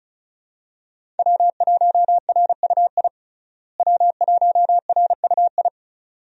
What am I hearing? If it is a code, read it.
Morse code, heard as a keyed tone, about 34 wpm, 700 Hz. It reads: W1RUI W1RUI
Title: USA Call Signs - 5 Characters - Repeated 2x - 35wpm